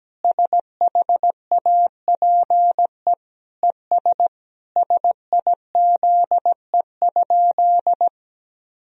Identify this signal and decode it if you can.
Morse code, heard as a keyed tone, about 17 wpm, 700 Hz. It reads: SHAPE ES SIZE?
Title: Sentences from Top 500 Words and Abbreviations 17wpm